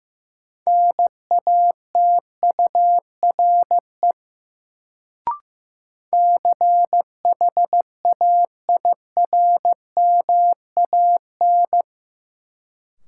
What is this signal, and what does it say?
Morse code, heard as a keyed tone, about 15 wpm, 700 Hz. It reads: NATURE CHAIRMAN
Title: Sets of 3 Words - New Word Drill - 15wpm